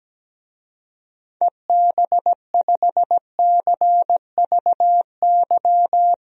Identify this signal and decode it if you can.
Morse code, heard as a keyed tone, about 17 wpm, 700 Hz. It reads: EB5CVY
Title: DXCC-100 Call Signs - 6 Characters - Repeated 2x - 17wpm